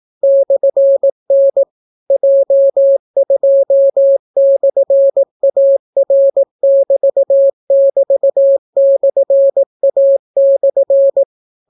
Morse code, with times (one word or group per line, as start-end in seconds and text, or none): 0.2-1.6 /N
2.1-11.2 J2/AR<BT><BT>/A/